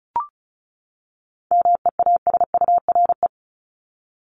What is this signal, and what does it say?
Morse code, heard as a keyed tone, about 34 wpm, 700 Hz. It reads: MEASURE
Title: Top 500 Words 35wpm